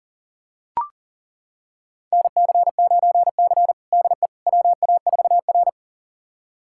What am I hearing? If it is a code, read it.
Morse code, heard as a keyed tone, about 37 wpm, 700 Hz. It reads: NC9C DE WA4R
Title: USA Callsign DE Callsign - 4 Characters - 40wpm